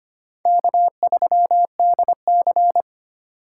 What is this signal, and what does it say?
Morse code, heard as a keyed tone, about 25 wpm, 700 Hz. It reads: K3DC
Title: USA Call Signs - 4 Characters - 4x Character Spacing - 25wpm